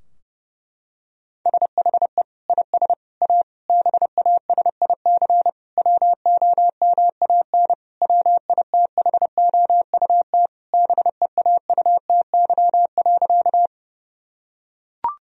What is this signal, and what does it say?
Morse code, heard as a keyed tone, about 30 wpm, 700 Hz. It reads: SHE IS A BASIC WOMAN WITHOUT BEAUTY.
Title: Sentences from Top 1000 Words - Review Part 3 - 30wpm